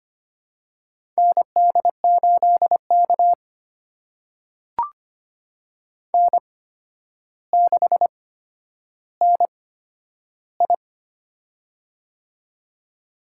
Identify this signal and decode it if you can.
Morse code, heard as a keyed tone, about 25 wpm, spaced as for 11 wpm, 700 Hz. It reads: ND8K N6NI
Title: USA Call Signs - 4 Characters - 8x Character Spacing - 25wpm